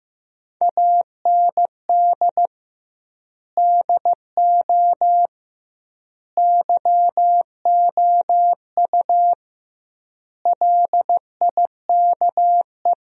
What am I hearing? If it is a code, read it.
Morse code, heard as a keyed tone, about 15 wpm, 700 Hz. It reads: AND DO YOU LIKE
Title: 2x Word-Spacing - Sentences from Top 100 Words - 15wpm